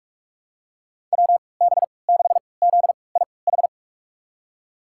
Morse code, heard as a keyed tone, about 42 wpm, spaced as for 25 wpm, 700 Hz. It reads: WB6ZIH
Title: USA Call Signs - 6 Characters - 3x Character Spacing - 45wpm